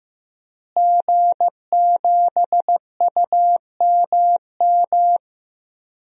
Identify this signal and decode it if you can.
Morse code, heard as a keyed tone, about 15 wpm, 700 Hz. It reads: G7UMM
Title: DXCC-100 Call Signs - 5 Characters - Repeated 2x - 15wpm